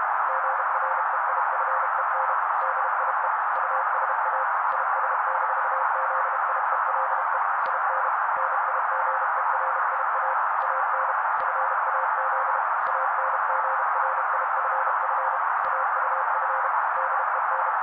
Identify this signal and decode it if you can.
Morse code, heard as a keyed tone, about 32 wpm, 530 Hz. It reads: GREIFEN DIE USA IS-ZIELE IN NIGERIA AN FAZ ANGRIFF AUF DEN
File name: daily / 31wpm noise